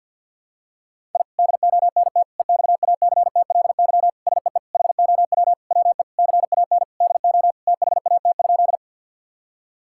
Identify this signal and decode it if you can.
Morse code, heard as a keyed tone, about 51 wpm, 700 Hz. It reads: I DONT EXACTLY SEE HOW WE CAN DO THAT?
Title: Sentences from Top 1000 Words - Review Part 6 - 50wpm